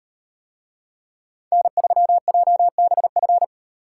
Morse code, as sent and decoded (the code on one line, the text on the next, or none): -. ...-- .--- -... ..-.
N3JBF